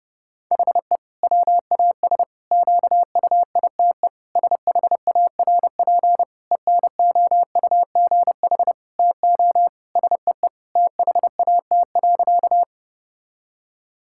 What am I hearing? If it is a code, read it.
Morse code, heard as a keyed tone, about 32 wpm, 700 Hz. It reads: HE WAS QUITE SHARP ENOUGH TO SEE THAT.
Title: Sentences from Top 1600 Words 30wpm